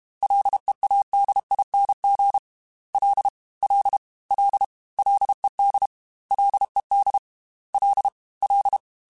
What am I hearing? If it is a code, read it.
Morse code, heard as a keyed tone, about 32 wpm, 800 Hz. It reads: LEADING L L L LED LED L L
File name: Ls / Single L m3s / Single L @30